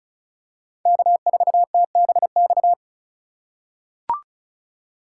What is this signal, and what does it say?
Morse code, heard as a keyed tone, about 34 wpm, 700 Hz. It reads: K4TBX
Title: USA Call Signs - 5 Characters - 2x Character Spacing - 35wpm